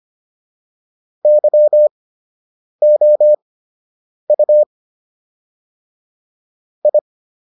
Morse code, heard as a keyed tone, about 25 wpm, spaced as for 8 wpm, 600 Hz. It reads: YOU I